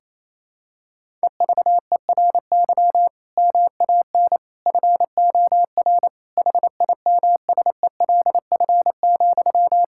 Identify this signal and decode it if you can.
Morse code, heard as a keyed tone, about 27 wpm, 700 Hz. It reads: EVERY MAN FOR HIMSELF,
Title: Sentences from Top 1600 Words 28wpm